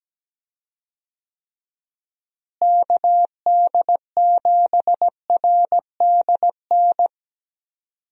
Morse code, heard as a keyed tone, about 17 wpm, 700 Hz. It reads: KD7RDN
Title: USA Call Signs - 6 Characters - Repeated 3x - 17wpm